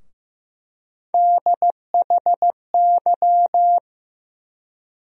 Morse code, heard as a keyed tone, about 15 wpm, 700 Hz. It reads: DHY
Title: USA Call Sign Postfixes - 3 Letters 15wpm